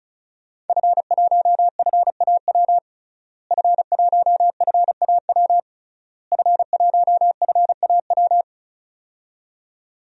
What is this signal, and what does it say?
Morse code, heard as a keyed tone, about 34 wpm, 700 Hz. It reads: F1FAW F1FAW F1FAW
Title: DXCC-100 Call Signs - 5 Characters - Repeated 3x - 35wpm